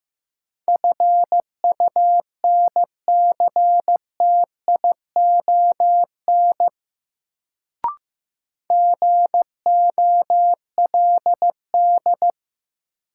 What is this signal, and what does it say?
Morse code, heard as a keyed tone, about 15 wpm, 700 Hz. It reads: FUNCTION GOLD
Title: Sentences from Top 1100 Words - New Word Drill Alt - 15wpm